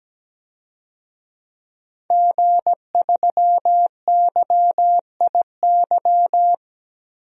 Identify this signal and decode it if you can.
Morse code, heard as a keyed tone, about 17 wpm, 700 Hz. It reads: G3YIY
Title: DXCC-100 Call Signs - 5 Characters - Repeated 2x - 17wpm